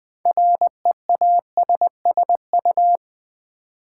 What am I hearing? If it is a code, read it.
Morse code, heard as a keyed tone, about 20 wpm, 700 Hz. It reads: REASSU
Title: Build A Word - 3 Letters At A Time - 8-Letter Words - 20wpm